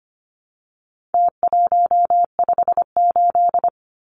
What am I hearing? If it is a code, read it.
Morse code, heard as a keyed tone, about 25 wpm, 700 Hz. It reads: T158
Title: Numbers - Contest Style - 4-Digit 25wpm